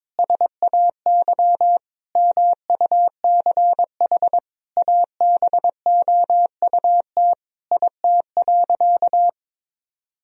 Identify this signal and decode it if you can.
Morse code, heard as a keyed tone, about 22 wpm, 700 Hz. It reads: SAY MUCH ABOUT IT.
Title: Sentences from Top 1000 Words - Review Part 2 - 22wpm